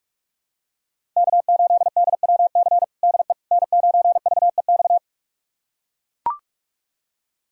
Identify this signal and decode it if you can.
Morse code, heard as a keyed tone, about 41 wpm, 700 Hz. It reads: K8DWC DE N9VEX
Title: USA Callsign DE Callsign - 5 Characters - 45wpm